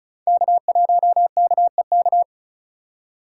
Morse code, heard as a keyed tone, about 34 wpm, 700 Hz. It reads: K1KEK
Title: USA Call Signs - 5 Characters - 3x Character Spacing - 35wpm